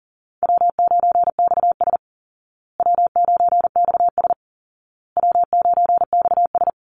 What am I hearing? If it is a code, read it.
Morse code, heard as a keyed tone, about 40 wpm, 700 Hz. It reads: W9XS W9XS W9XS
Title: Call Signs Repeated 2x - 40wpm